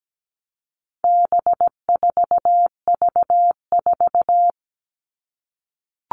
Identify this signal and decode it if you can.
Morse code, heard as a keyed tone, about 17 wpm, 700 Hz. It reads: B4V4
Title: All Cut Numbers - 4-Digit 17wpm